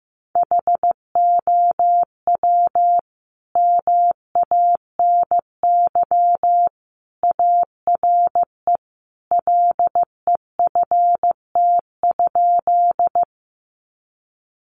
Morse code, heard as a keyed tone, about 15 wpm, 700 Hz. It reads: HOW MANY ARE LEFT?
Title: Sentences from Top 300 Words 15wpm